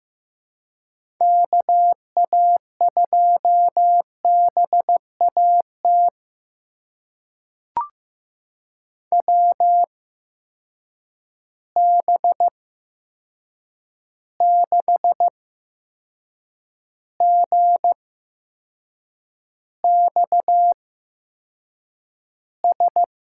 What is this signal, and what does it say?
Morse code, heard as a keyed tone, about 15 wpm, spaced as for 6 wpm, 700 Hz. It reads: KA2BAT WB6GXS
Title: USA Call Signs - 6 Characters - 8x Character Spacing - 15wpm